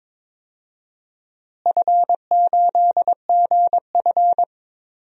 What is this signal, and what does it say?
Morse code, heard as a keyed tone, about 22 wpm, 700 Hz. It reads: F8GF
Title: DXCC-100 Call Signs - 4 Characters - Repeated 2x - 22wpm